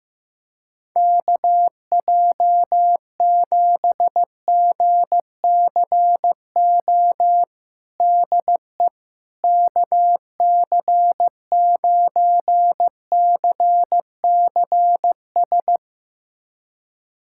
Morse code, heard as a keyed tone, about 15 wpm, 700 Hz. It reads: KJ7GCO DE KC9CCS
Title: USA Callsign DE Callsign 15wpm